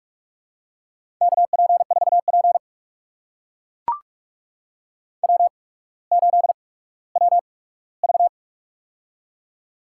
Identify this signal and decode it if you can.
Morse code, heard as a keyed tone, about 44 wpm, spaced as for 19 wpm, 690 Hz. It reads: KP4P W8WV
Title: USA Call Signs - 4 Characters - 8x Character Spacing - 45wpm